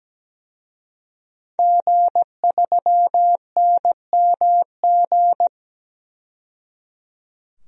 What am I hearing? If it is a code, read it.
Morse code, heard as a keyed tone, about 17 wpm, 700 Hz. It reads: G3NMG
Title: DXCC-100 Call Signs - 5 Characters - 17wpm